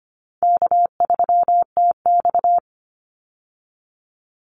Morse code, heard as a keyed tone, about 25 wpm, 700 Hz. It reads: K3TX
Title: Call Signs - 25wpm